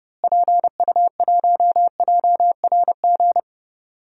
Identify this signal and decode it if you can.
Morse code, heard as a keyed tone, about 30 wpm, 700 Hz. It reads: PU1JRG
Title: DXCC-100 Call Signs - 6 Characters - Repeated 2x - 30wpm